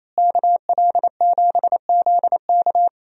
Morse code, heard as a keyed tone, about 28 wpm, 700 Hz. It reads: KL7ZK
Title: USA Call Signs - 5 Characters - Repeated 3x - 28wpm